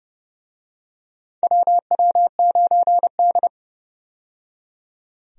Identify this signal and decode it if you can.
Morse code, heard as a keyed tone, about 30 wpm, 700 Hz. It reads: WW9D
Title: USA Call Signs - 4 Characters - 30wpm